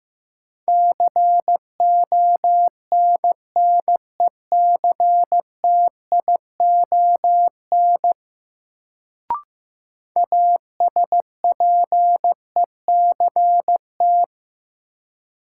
Morse code, heard as a keyed tone, about 15 wpm, 700 Hz. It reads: CONNECTION ASPECT
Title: Sentences from Top 1300 Words - New Word Drill - 15wpm